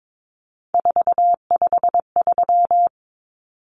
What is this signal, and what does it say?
Morse code, heard as a keyed tone, about 22 wpm, 700 Hz. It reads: 453
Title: Common Cut Numbers - 3-Digit 22wpm